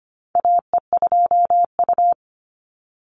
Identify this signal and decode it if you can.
Morse code, heard as a keyed tone, about 25 wpm, 700 Hz. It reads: AE2U
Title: Call Signs - 25wpm